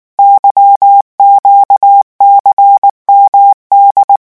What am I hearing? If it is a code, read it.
Morse code, heard as a keyed tone, about 19 wpm, 800 Hz. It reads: YQCMD
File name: morse mp3 / revision13-vitesse 18 mots